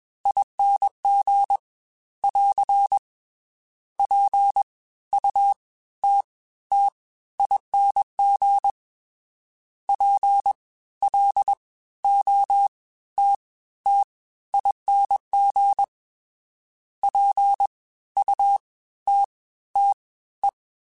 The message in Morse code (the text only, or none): ING <AR> PUTTING PLOTTING PUTTE